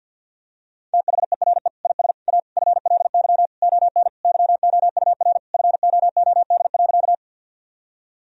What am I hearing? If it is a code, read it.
Morse code, heard as a keyed tone, about 50 wpm, 700 Hz. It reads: THERE IS A FLY ON YOUR FOOD.
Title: Sentences from Top 1000 Words - Review Part 2 - 50wpm